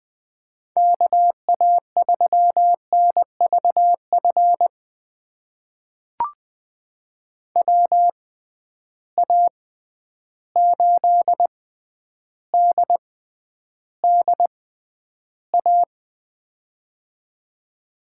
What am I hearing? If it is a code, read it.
Morse code, heard as a keyed tone, about 21 wpm, spaced as for 10 wpm, 700 Hz. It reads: KA3NVF WA8DDA
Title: USA Call Signs - 6x Character Spacing - 20wpm